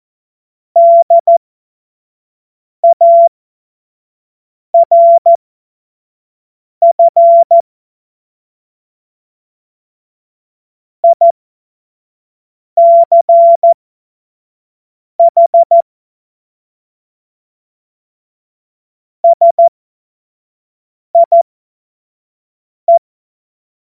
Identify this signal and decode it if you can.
Morse code, heard as a keyed tone, about 14 wpm, spaced as for 5 wpm, 680 Hz. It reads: DARF ICH SIE
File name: Morsekurs / Witze / Witz 11. Januar 2022, 25 F70